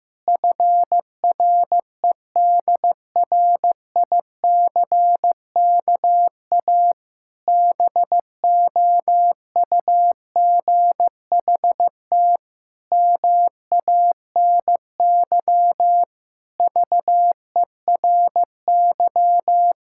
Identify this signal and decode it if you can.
Morse code, heard as a keyed tone, about 14 wpm, 700 Hz. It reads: FREDRICKA BOUGHT MANY VERY
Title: Pangram Challenge v2 - 15wpm